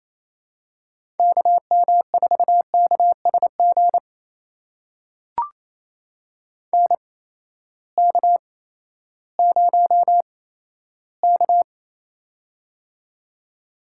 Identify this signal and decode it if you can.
Morse code, heard as a keyed tone, about 28 wpm, spaced as for 14 wpm, 700 Hz. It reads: KM4KSG NK0K
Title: USA Call Signs - 8x Character Spacing - 28wpm